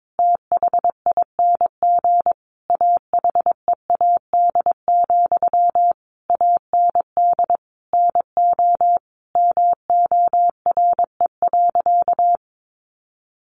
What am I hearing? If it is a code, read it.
Morse code, heard as a keyed tone, about 22 wpm, 700 Hz. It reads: THING AHEAD, AND NO MORE.
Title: Sentences from Top 1100 Words 22wpm